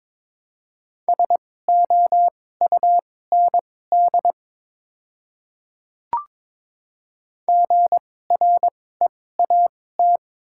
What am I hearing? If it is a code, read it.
Morse code, heard as a keyed tone, about 21 wpm, spaced as for 16 wpm, 700 Hz. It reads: SOUND GREAT
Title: 2x Character Spacing - Top 200 Words - 22wpm